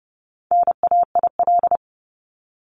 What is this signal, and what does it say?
Morse code, heard as a keyed tone, about 30 wpm, 700 Hz. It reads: NAIL
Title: Top 1501-1750 Words 30wpm